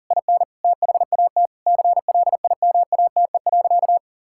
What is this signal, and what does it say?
Morse code, heard as a keyed tone, about 40 wpm, 700 Hz. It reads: IN THAT CLIMATE.
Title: Sentences from Top 1600 Words 40wpm